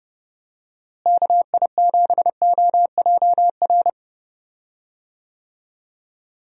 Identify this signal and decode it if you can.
Morse code, heard as a keyed tone, about 29 wpm, 700 Hz. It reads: KI7OJR
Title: USA Call Signs - 6 Characters - Repeated 2x - 30wpm